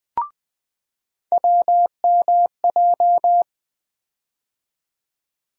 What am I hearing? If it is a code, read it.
Morse code, heard as a keyed tone, about 20 wpm, 700 Hz. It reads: WMJ
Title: USA Call Sign Postfixes - 3 Letters 20wpm